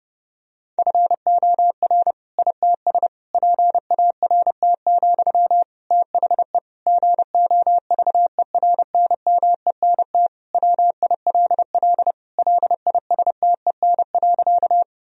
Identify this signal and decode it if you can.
Morse code, heard as a keyed tone, about 30 wpm, 700 Hz. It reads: FOR ITS PART, THE GOVERNMENT WILL LISTEN.
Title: Sentences from Top 1000 Words - Review Part 3 - 30wpm